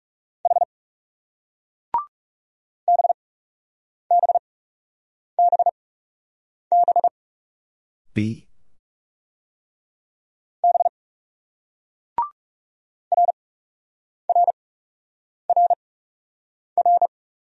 Morse code, heard as a keyed tone, about 36 wpm, spaced as for 8 wpm, 700 Hz. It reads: H BBBB B RRRR
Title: Speed-Racing Single Letters 30wpm